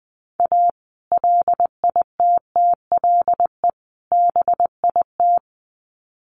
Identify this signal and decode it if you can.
Morse code, heard as a keyed tone, about 19 wpm, 700 Hz. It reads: A LITTLE BIT
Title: Sets of 3 Words 20wpm